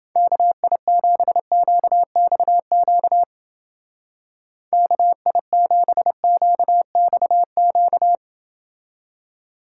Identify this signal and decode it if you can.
Morse code, heard as a keyed tone, about 28 wpm, 700 Hz. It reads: KI7QXQ KI7QXQ